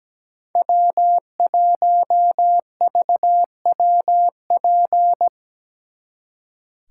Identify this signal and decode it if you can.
Morse code, heard as a keyed tone, about 20 wpm, 700 Hz. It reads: W1VWP
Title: Speed Racing - USA 1x3 Call Signs 17wpm